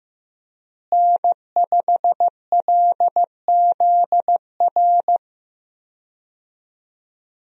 Morse code, heard as a keyed tone, about 16 wpm, 700 Hz. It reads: N5LZR
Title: USA Call Signs - 5 Characters - Repeated 3x - 15wpm